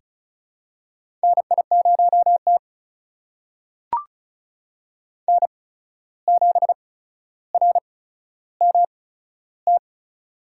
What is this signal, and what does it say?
Morse code, heard as a keyed tone, about 36 wpm, spaced as for 14 wpm, 700 Hz. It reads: NI0T N7RMT
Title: USA Call Signs - 8x Character Spacing - 35wpm